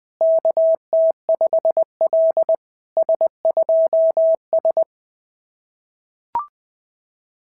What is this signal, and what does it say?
Morse code, heard as a keyed tone, about 20 wpm, 690 Hz. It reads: KT5L S2S
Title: SOTA QSO Practice 20wpm